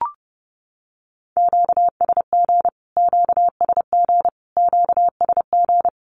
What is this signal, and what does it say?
Morse code, heard as a keyed tone, about 30 wpm, 700 Hz. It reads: QSG QSG QSG
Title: Q-Codes Repeated 3x 30wpm